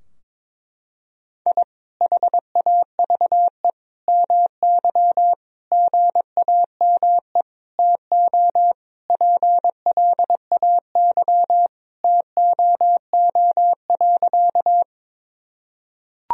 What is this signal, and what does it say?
Morse code, heard as a keyed tone, about 22 wpm, 700 Hz. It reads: I HAVE MY GAME TO PLAY TOO.
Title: Sentences from Top 1000 Words - Review Part 3 - 22wpm